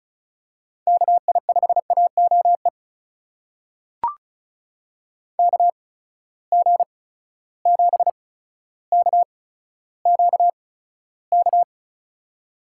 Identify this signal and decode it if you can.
Morse code, heard as a keyed tone, about 36 wpm, spaced as for 15 wpm, 700 Hz. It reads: KI5AOE KG7KQK
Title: USA Call Signs - 8x Character Spacing - 35wpm